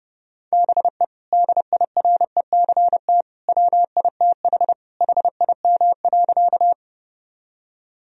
Morse code, heard as a keyed tone, about 30 wpm, 700 Hz. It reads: BE DIRECT WITH HIM.
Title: Sentences from Top 1000 Words - Review Part 2 - 30wpm